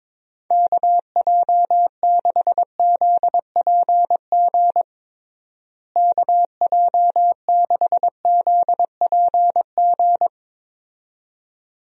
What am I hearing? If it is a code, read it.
Morse code, heard as a keyed tone, about 22 wpm, 700 Hz. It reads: KJ6ZPG KJ6ZPG
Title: USA Call Signs - 6 Characters - Repeated 2x - 22wpm